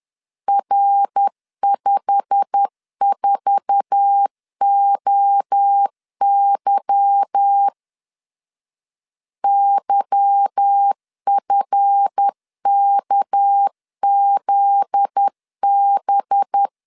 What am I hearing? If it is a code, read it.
Morse code, heard as a keyed tone, about 10 wpm, 800 Hz. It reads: R54OY YFKZB
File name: LEZIONI / 10 SIGNS / 10B INT 10wpm 2844